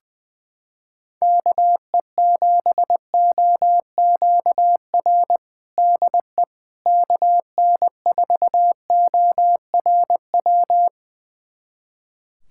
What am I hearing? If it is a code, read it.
Morse code, heard as a keyed tone, about 20 wpm, 700 Hz. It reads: KE7OQR DE KN4ORW
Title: USA Callsign DE Callsign - 6 Characters - 20wpm